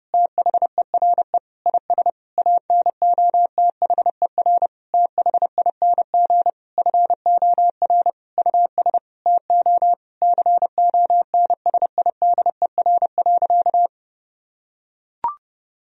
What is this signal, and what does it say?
Morse code, heard as a keyed tone, about 30 wpm, 700 Hz. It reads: THERE IS ANOTHER THING FOR US TO CONSIDER.
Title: Sentences from Top 1000 Words - Review Part 4 - 30wpm